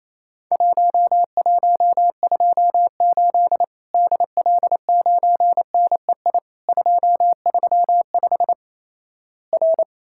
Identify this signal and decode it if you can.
Morse code, heard as a keyed tone, about 28 wpm, 700 Hz. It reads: 1128 DL9NEI 235 R
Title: Worked All Europe DX Contest - QTC Exchange - 28wpm